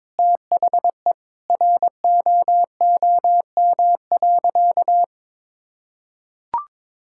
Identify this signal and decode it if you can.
Morse code, heard as a keyed tone, about 21 wpm, 700 Hz. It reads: THE ROOM.
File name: practice / 22wpm / Sentences with 2-Word Phrases - 22wpm